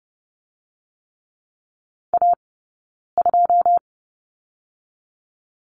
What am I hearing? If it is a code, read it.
Morse code, heard as a keyed tone, about 30 wpm, 700 Hz. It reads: A 2
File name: course / 30:10wpm / 086 Lesson 15d - Number 5 - Two Characters 30wpm